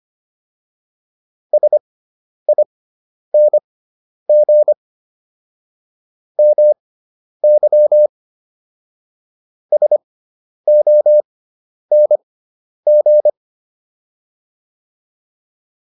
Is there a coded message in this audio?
Yes: Morse code from 1.5 to 13.3 s, heard as a keyed tone, about 25 wpm, spaced as for 10 wpm, 600 Hz. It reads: SING MY SONG